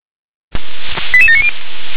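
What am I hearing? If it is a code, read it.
Five-tone selective call (ZVEI), heard as quick runs of five tones, tones of about 70 ms. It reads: 88790
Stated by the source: ZVEI five-tone selective call received with an SDR.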